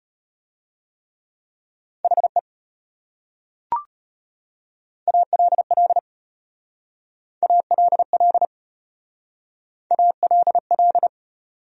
Morse code, heard as a keyed tone, about 33 wpm, 700 Hz. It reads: HE ALL ALL ALL